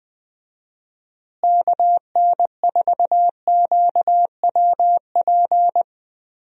Morse code, heard as a keyed tone, about 20 wpm, 700 Hz. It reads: KN4QWP
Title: USA Call Signs - 6 Characters - Repeated 3x - 20wpm